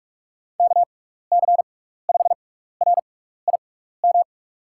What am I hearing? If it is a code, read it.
Morse code, heard as a keyed tone, about 44 wpm, spaced as for 15 wpm, 690 Hz. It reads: KC5RIM
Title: USA Call Signs - 6x Character Spacing - 45wpm